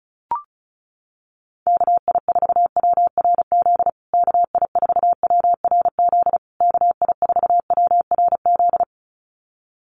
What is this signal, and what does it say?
Morse code, heard as a keyed tone, about 34 wpm, 700 Hz. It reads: KI4WRZ KI4WRZ KI4WRZ
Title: USA Call Signs Repeated 3x 35wpm